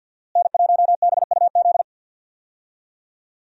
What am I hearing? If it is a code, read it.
Morse code, heard as a keyed tone, about 47 wpm, 700 Hz. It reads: N1BUZ
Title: USA Call Signs - 5 Characters - Repeated 2x - 50wpm